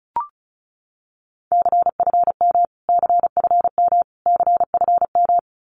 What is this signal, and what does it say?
Morse code, heard as a keyed tone, about 34 wpm, 700 Hz. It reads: CFM CFM CFM
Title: Abbreviations Repeated 3x - 35wpm